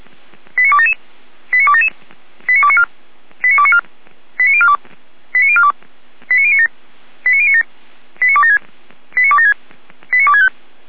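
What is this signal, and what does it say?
Five-tone selective call (ZVEI), heard as quick runs of five tones, tones of about 70 ms. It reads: 89277 89277 89294 89294 89042 89042 89097 89097 89276 89276 89265
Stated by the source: ZVEI five-tone selective call 89310 received by SDR, fire brigade Orlamünde (feuerwehr-orlamuende).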